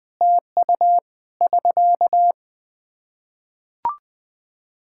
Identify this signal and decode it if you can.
Morse code, heard as a keyed tone, about 20 wpm, 700 Hz. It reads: TU <SK>
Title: Practice QSO - Exchange 4 - Ending - 20wpm